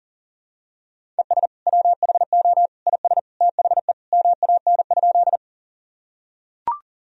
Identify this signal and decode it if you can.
Morse code, heard as a keyed tone, about 40 wpm, 700 Hz. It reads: ES WHO IS THE MAN?
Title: Sentences from Top 200 Words and Abbreviations 40wpm